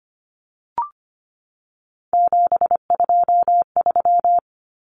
Morse code, heard as a keyed tone, about 25 wpm, 700 Hz. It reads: 723
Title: Common Cut Numbers - 3-Digit 25wpm